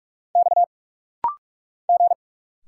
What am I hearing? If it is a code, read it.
Morse code, heard as a keyed tone, about 45 wpm, 700 Hz. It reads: X G
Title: Single Letter-Number Alt - Rapid-Fire 45wpm